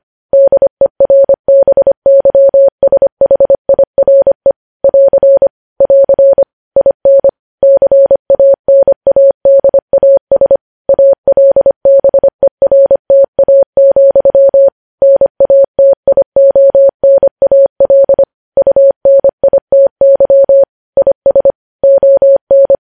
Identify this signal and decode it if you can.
Morse code, heard as a keyed tone, about 25 wpm, 550 Hz. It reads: DERBYSHIRE <AR> <AR> IN CANADAS ALBERTA, NATIONAL UNITY IS ON
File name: podcast / News headlines-1x-25x25